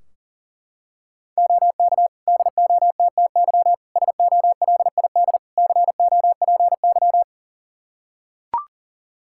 Morse code, heard as a keyed tone, about 40 wpm, 700 Hz. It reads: OK DOTTY SOLID COPY
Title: K1USN SST QSO Practice 40wpm